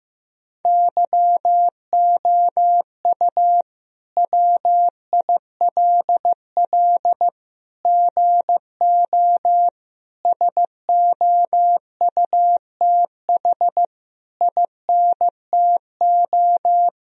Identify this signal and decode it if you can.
Morse code, heard as a keyed tone, about 15 wpm, 700 Hz. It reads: YOU WILL GO SOUTH INTO
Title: Sentences from Top 1000 Words - Review Part 2 - 15wpm